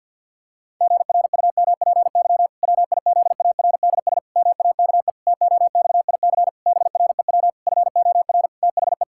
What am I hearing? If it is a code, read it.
Morse code, heard as a keyed tone, about 52 wpm, 700 Hz. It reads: GRUMPY WIZARDS MAKE TOXIC BREW FOR THE
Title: Pangram Challenge v2 - 50wpm